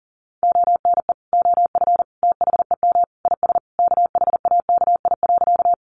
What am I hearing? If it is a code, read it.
Morse code, heard as a keyed tone, about 40 wpm, 700 Hz. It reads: ONE OF THEM IS KHAKI.
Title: Additional Easy Phrases 40wpm